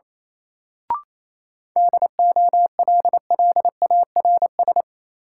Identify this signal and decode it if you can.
Morse code, heard as a keyed tone, about 27 wpm, 700 Hz. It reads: DOLLARS
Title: Sets of 5 Words - New Word Drill - 28wpm